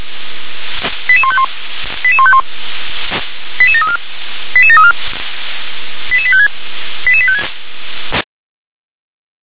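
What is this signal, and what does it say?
Five-tone selective call (ZVEI), heard as quick runs of five tones, tones of about 70 ms. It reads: 88151 88151 88734 88734 88856 88856
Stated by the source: ZVEI five-tone selective call 88509 received by SDR.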